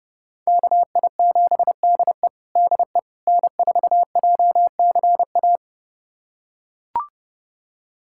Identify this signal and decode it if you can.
Morse code, heard as a keyed tone, about 30 wpm, 700 Hz. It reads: KI7DE DE N4JCA
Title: USA Callsign DE Callsign - 5 Characters - 30wpm